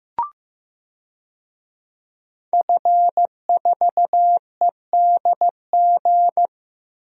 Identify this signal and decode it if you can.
Morse code, heard as a keyed tone, about 15 wpm, 700 Hz. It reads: F4EDG
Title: DXCC-100 Call Signs - 5 Characters - 15wpm